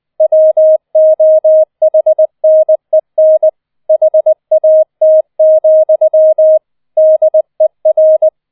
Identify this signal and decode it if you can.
Morse code, heard as a keyed tone, about 20 wpm, 630 Hz. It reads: WOHNEN HAT, DER